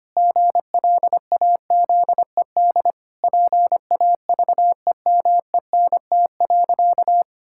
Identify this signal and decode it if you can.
Morse code, heard as a keyed tone, about 25 wpm, 700 Hz. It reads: GLAZED PAVEMENT.
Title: Pangram Challenge v2 - 25wpm